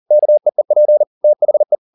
Morse code, heard as a keyed tone, about 40 wpm, 600 Hz. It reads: KEEP THE
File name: the old man and the sea 2 40wpm